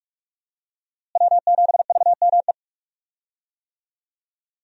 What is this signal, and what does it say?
Morse code, heard as a keyed tone, about 44 wpm, 700 Hz. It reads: W7VME
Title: USA Call Signs - 5 Characters - Repeated 3x - 45wpm